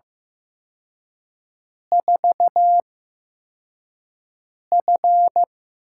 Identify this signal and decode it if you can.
Morse code, heard as a keyed tone, about 15 wpm, spaced as for 4 wpm, 700 Hz. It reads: 4F
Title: USA Call Signs - 4 Characters - 8x Character Spacing - 15wpm